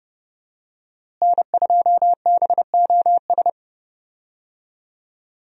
Morse code, heard as a keyed tone, about 32 wpm, 700 Hz. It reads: N2BOS
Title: USA Call Signs - 5 Characters - Repeated 3x - 30wpm